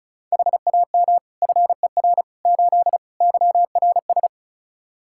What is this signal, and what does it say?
Morse code, heard as a keyed tone, about 34 wpm, 700 Hz. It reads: HAM FER 8 YRS
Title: Practice QSO - Exchange 3 - More Info - 35wpm